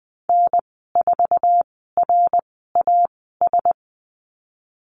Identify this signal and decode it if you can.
Morse code, heard as a keyed tone, about 20 wpm, spaced as for 15 wpm, 700 Hz. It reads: N4RAS
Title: 055 Lesson 10g - Letter r - Call Signs 20wpm